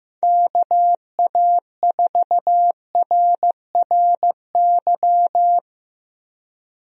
Morse code, heard as a keyed tone, about 15 wpm, 700 Hz. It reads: KA4RRY
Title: USA Call Signs - 6 Characters - Repeated 2x - 15wpm